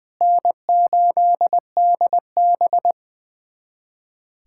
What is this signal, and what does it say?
Morse code, heard as a keyed tone, about 21 wpm, 700 Hz. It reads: N8DB
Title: USA Callsign DE Callsign - 4 Characters - 20wpm